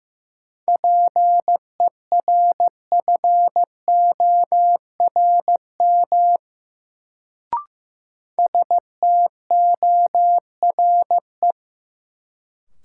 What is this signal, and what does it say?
Morse code, heard as a keyed tone, about 15 wpm, 700 Hz. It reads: PERFORM STORE
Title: Sentences from Top 1000 Words - Review Part 5 - New Word Drill - 15wpm